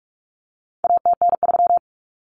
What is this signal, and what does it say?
Morse code, heard as a keyed tone, about 44 wpm, 700 Hz. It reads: ATN3
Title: Common Cut Numbers - 4-Digit 45wpm